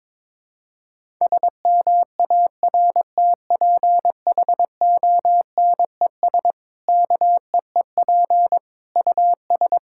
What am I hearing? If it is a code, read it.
Morse code, heard as a keyed tone, about 22 wpm, 700 Hz. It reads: SMARTPHONES KEEP US
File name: Difficult Sentences with Hints - 22wpm